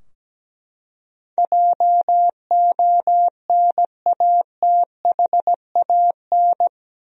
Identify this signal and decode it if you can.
Morse code, heard as a keyed tone, about 17 wpm, 700 Hz. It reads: JONATHAN
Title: USA Common Names - 17wpm